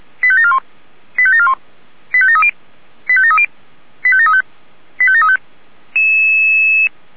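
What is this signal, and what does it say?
Five-tone selective call (ZVEI), heard as quick runs of five tones, tones of about 70 ms. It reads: 85741 85741 85730 85730 85735 85735
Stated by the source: ZVEI five-tone call received by SDR.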